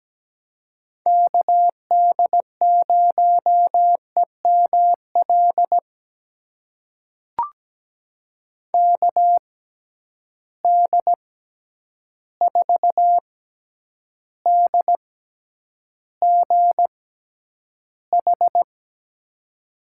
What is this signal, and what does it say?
Morse code, heard as a keyed tone, about 17 wpm, spaced as for 9 wpm, 700 Hz. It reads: KD0EML KD4DGH
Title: USA Call Signs - 6 Characters - 6x Character Spacing - 17wpm